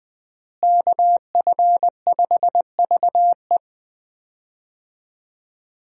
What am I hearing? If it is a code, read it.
Morse code, heard as a keyed tone, about 20 wpm, 700 Hz. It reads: KF5VE
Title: USA Call Signs - 5 Characters - Repeated 2x - 20wpm